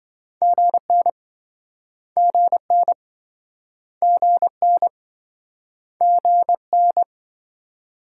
Morse code, heard as a keyed tone, about 28 wpm, 700 Hz. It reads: GN GN GN GN